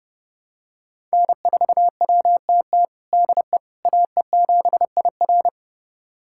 Morse code, heard as a keyed tone, about 30 wpm, 700 Hz. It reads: N4WTT DE AE7IR
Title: USA Callsign DE Callsign - 5 Characters - 30wpm